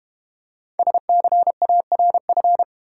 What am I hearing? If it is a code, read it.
Morse code, heard as a keyed tone, about 30 wpm, 700 Hz. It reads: SCARF
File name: Easy Words - Speed-Racing with Hints - 25wpm